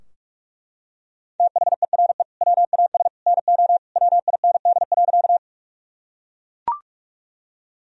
Morse code, heard as a keyed tone, about 42 wpm, 700 Hz. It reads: THERE WAS NO WIND.
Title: Sentences from Top 1000 Words - Review Part 2 - 45wpm